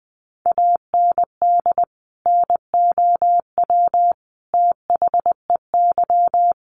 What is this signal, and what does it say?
Morse code, heard as a keyed tone, about 20 wpm, 700 Hz. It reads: AND NOW THEY
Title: Sentences from Top 1100 Words 20wpm